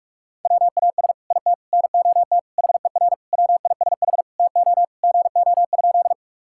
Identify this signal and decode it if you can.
Morse code, heard as a keyed tone, about 41 wpm, 700 Hz. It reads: WAS IT NOT HER WISH TO GO?
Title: Sentences from Top 1600 Words 45wpm